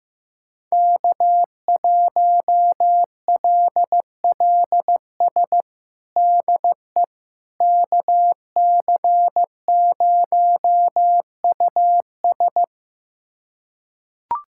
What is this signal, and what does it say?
Morse code, heard as a keyed tone, about 15 wpm, 700 Hz. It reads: K1LLS DE KC0US
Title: USA Callsign DE Callsign - 5 Characters - 15wpm